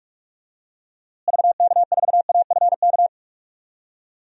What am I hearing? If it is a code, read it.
Morse code, heard as a keyed tone, about 46 wpm, 700 Hz. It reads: VK4AFK
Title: DXCC-100 Call Signs - 6 Characters - Repeated 3x - 45wpm